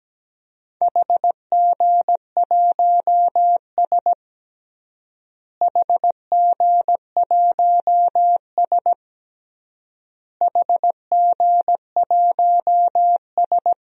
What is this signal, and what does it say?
Morse code, heard as a keyed tone, about 17 wpm, 700 Hz. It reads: HG1S HG1S HG1S
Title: DXCC-100 Call Signs - 4 Characters - Repeated 3x - 17wpm